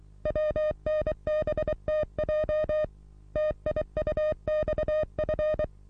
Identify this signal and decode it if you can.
Morse code, heard as a keyed tone, about 23 wpm, 610 Hz. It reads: WNBTJ TIUXF
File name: repos / Morsekurs / DJ4UF Morsen27